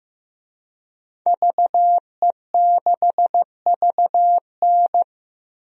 Morse code, heard as a keyed tone, about 15 wpm, 700 Hz. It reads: VE6VN
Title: DXCC-100 Call Signs - 5 Characters - 15wpm